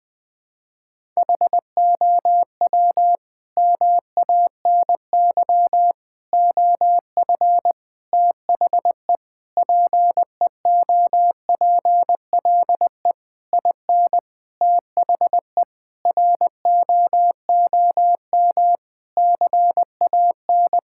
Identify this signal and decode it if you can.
Morse code, heard as a keyed tone, about 20 wpm, 700 Hz. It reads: HOW MANY OF THE PEOPLE IN THE ROOM CAN
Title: Sentences from Top 1000 Words - Review Part 6 - 20wpm